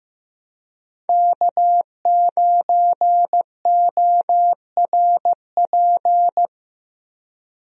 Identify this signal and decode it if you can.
Morse code, heard as a keyed tone, about 15 wpm, 700 Hz. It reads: K9ORP
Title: USA Call Signs - 5 Characters - Repeated 2x - 15wpm